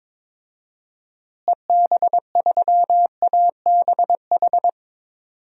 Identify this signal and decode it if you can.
Morse code, heard as a keyed tone, about 22 wpm, 700 Hz. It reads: EB3ABH
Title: DXCC-100 Call Signs - 6 Characters - Repeated 3x - 22wpm